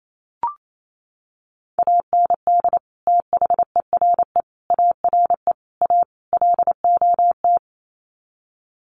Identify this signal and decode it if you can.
Morse code, heard as a keyed tone, about 28 wpm, 700 Hz. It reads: AND THERE ARE A LOT
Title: Sets of 5 Words Encore2 28wpm